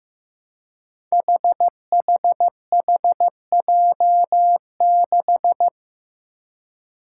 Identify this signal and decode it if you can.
Morse code, heard as a keyed tone, about 15 wpm, 700 Hz. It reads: HHHJ6
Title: Five Challenging Letter-Number Combinations 15wpm